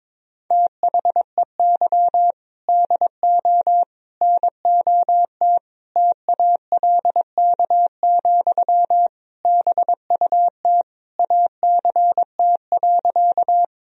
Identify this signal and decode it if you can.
Morse code, heard as a keyed tone, about 22 wpm, 700 Hz. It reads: THEY DO NOT TALK, BUT ACT.
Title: Sentences from Top 1000 Words - Review Part 1 - 22wpm